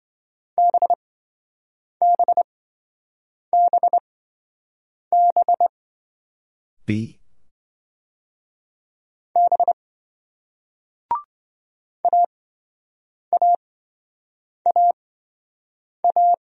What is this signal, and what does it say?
Morse code, heard as a keyed tone, about 28 wpm, spaced as for 7 wpm, 700 Hz. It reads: BBBB B AAAA